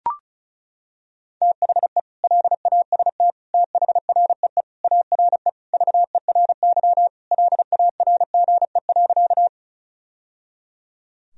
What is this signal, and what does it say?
Morse code, heard as a keyed tone, about 36 wpm, 700 Hz. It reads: THE LAST THREE ARE VERY LARGE.
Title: Sentences from Top 1000 Words - Review Part 1 - 35wpm